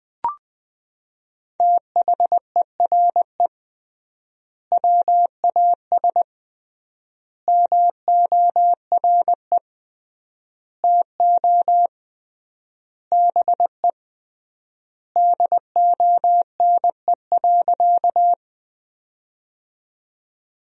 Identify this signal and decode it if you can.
Morse code, heard as a keyed tone, about 21 wpm, 700 Hz. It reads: THERE WAS MORE TO BE DONE.
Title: 3x Word-Spacing - Sentences from Top 500 Words - 20wpm